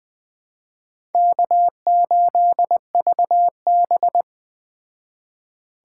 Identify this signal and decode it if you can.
Morse code, heard as a keyed tone, about 21 wpm, 700 Hz. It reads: K8VB